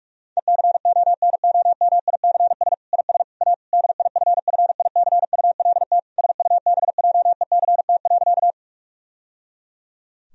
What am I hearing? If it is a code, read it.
Morse code, heard as a keyed tone, about 46 wpm, 700 Hz. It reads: ECONOMICS IS A DIFFICULT SUBJECT.
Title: Sentences from Top 1000 Words - Review Part 3 - 45wpm